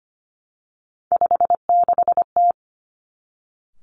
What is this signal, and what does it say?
Morse code, heard as a keyed tone, about 25 wpm, 700 Hz. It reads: 56T
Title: Three Challenging Letter-Number Combinations 25wpm